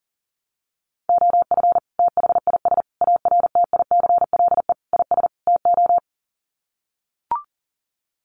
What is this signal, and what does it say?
Morse code, heard as a keyed tone, about 40 wpm, 700 Hz. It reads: OF THIS ARTICLE IS TO
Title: Sets of 5 Words Encore2 40wpm